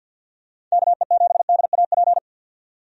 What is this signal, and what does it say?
Morse code, heard as a keyed tone, about 51 wpm, 700 Hz. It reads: KE7DAP
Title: USA Call Signs - 6 Characters - 50wpm